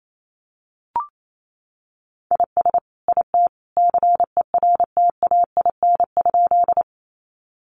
Morse code, heard as a keyed tone, about 28 wpm, 700 Hz. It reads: IS IT CERTAIN?
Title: Sentences from Top 500 Words 28wpm